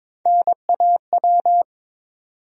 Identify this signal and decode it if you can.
Morse code, heard as a keyed tone, about 22 wpm, 700 Hz. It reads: NAW
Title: USA Call Sign Postfixes - 3 Letters 22wpm